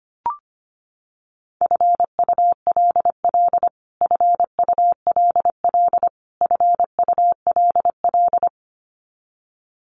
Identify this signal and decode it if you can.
Morse code, heard as a keyed tone, about 25 wpm, 700 Hz. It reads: FULL FULL FULL
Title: Top 500 Words Repeated 25wpm